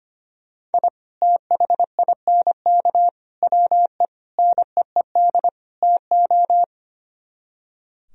Morse code, heard as a keyed tone, about 30 wpm, 700 Hz. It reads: I THINK WE NEED TO
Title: Speed-Racing - Sets of 5 Words 25wpm